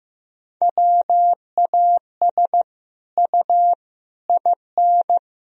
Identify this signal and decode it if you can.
Morse code, heard as a keyed tone, about 15 wpm, 700 Hz. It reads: WAS U IN
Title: Sentences from Top 300 Words and Abbreviations 15wpm